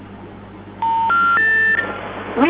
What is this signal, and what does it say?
Special information tone, with noise.